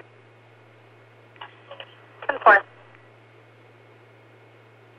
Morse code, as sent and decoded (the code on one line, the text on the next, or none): none